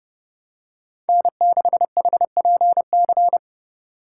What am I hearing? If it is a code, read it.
Morse code, heard as a keyed tone, about 30 wpm, 700 Hz. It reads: N6HPC
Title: USA Call Signs - 5 Characters - 6x Character Spacing - 30wpm